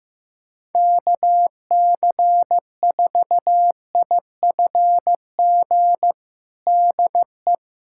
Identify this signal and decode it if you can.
Morse code, heard as a keyed tone, about 15 wpm, 700 Hz. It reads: KC4IFG DE
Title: USA Callsign DE Callsign - 6 Characters - 15wpm